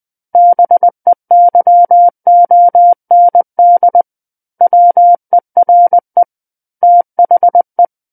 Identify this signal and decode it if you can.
Morse code, heard as a keyed tone, about 20 wpm, 700 Hz. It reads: BEYOND WERE THE
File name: 11-Chapter 11 - 20 WPM